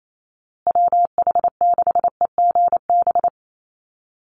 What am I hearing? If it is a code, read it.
Morse code, heard as a keyed tone, about 28 wpm, 700 Hz. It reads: WH6EGB
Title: USA Call Signs Repeated 2x 28wpm